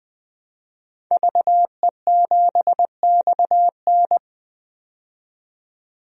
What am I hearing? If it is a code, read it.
Morse code, heard as a keyed tone, about 20 wpm, 700 Hz. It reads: VE7XN
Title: DXCC-100 Call Signs - 5 Characters - 20wpm